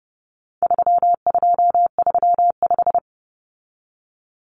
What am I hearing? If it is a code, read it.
Morse code, heard as a keyed tone, about 30 wpm, 700 Hz. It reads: 3235
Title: Numbers - 4-Digit 30wpm